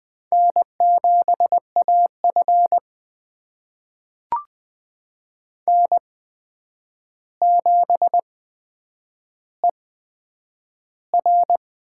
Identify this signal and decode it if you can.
Morse code, heard as a keyed tone, about 20 wpm, spaced as for 9 wpm, 700 Hz. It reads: N7AF N7ER
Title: USA Call Signs - 4 Characters - 8x Character Spacing - 20wpm